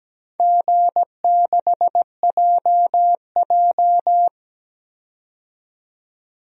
Morse code, heard as a keyed tone, about 17 wpm, 700 Hz. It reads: G6JJ
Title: DXCC-100 Call Signs - 4 Characters - 17wpm